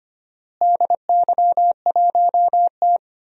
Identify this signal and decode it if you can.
Morse code, heard as a keyed tone, about 25 wpm, 700 Hz. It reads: DY1T
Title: DXCC-100 Call Signs - 4 Characters - Repeated 3x - 25wpm